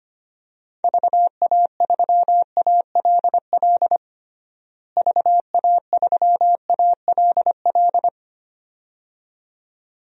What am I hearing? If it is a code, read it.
Morse code, heard as a keyed tone, about 25 wpm, 700 Hz. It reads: VA3ALL VA3ALL
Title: DXCC-100 Call Signs - 6 Characters - Repeated 2x - 25wpm